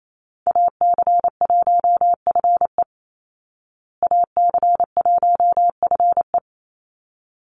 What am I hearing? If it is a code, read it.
Morse code, heard as a keyed tone, about 28 wpm, 700 Hz. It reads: AC1FE AC1FE
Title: Call Signs Repeated 3x - 28wpm